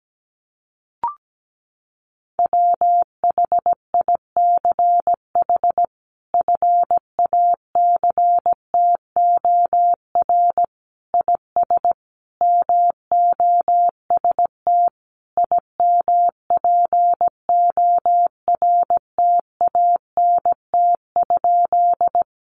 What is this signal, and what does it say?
Morse code, heard as a keyed tone, about 17 wpm, 700 Hz. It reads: WHICH FACTOR IS MOST IMPORTANT?
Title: Sentences from Top 1300 Words 17wpm